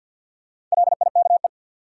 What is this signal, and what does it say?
Morse code, heard as a keyed tone, about 53 wpm, 700 Hz. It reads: LIKE
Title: Sentences from Top 100 Words - New Word Drill - 50wpm